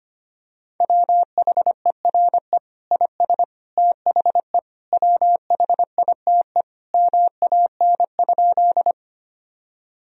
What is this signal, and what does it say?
Morse code, heard as a keyed tone, about 25 wpm, 700 Hz. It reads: WHERE IS THE WHITE MAN?
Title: Sentences from Top 1000 Words - Review Part 1 - 25wpm